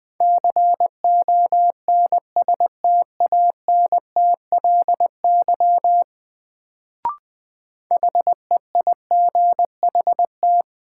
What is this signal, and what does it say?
Morse code, heard as a keyed tone, about 20 wpm, 700 Hz. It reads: CONSTANTLY HEIGHT
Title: Sentences from Top 1600 Words - New Word Drill Alt - 20wpm